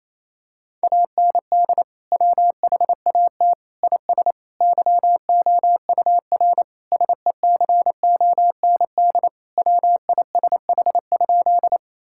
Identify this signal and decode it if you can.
Morse code, heard as a keyed tone, about 28 wpm, 700 Hz. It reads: AND WHAT IS YOUR SECOND WISH?
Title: Sentences from Top 1600 Words 28wpm